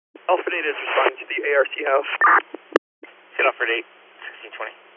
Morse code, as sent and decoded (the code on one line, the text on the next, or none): none